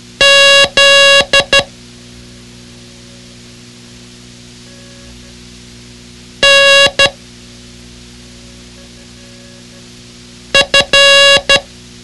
Morse code, heard as a keyed tone, about 12 wpm, 2800 Hz. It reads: Z N F
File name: ham radio / CodeQuick / Code Quick 06